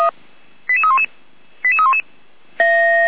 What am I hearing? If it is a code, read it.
Five-tone selective call (ZVEI), heard as quick runs of five tones, tones of about 70 ms. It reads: 88311 88311